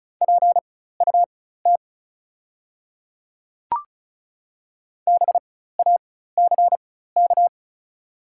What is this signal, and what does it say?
Morse code, heard as a keyed tone, about 36 wpm, spaced as for 16 wpm, 690 Hz. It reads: PUT BACK
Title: 4x Character Spacing - Top 500 Words - 35wpm